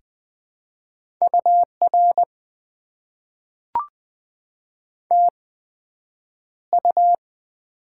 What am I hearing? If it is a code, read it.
Morse code, heard as a keyed tone, about 20 wpm, spaced as for 9 wpm, 700 Hz. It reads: UR TU
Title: Top 90 QSO Elements - 8x Character Spacing - 20wpm